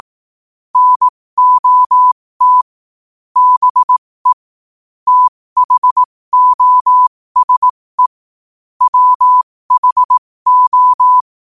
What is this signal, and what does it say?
Morse code, heard as a keyed tone, about 18 wpm, spaced as for 15 wpm, 1000 Hz. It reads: NOT BE THOSE WHO
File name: qotdfiles / 15wpm